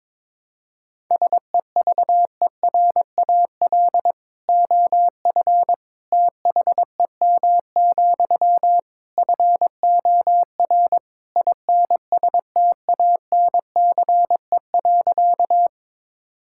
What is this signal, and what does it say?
Morse code, heard as a keyed tone, about 22 wpm, 700 Hz. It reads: SEVERAL OF THEM, FOR INSTANCE.
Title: Sentences from Top 1000 Words - Review Part 5 - 22wpm